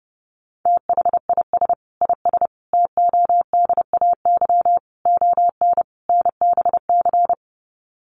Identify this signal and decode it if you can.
Morse code, heard as a keyed tone, about 30 wpm, 700 Hz. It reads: THIS IS TODAY ON NBC
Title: Sets of 5 Words Encore 30wpm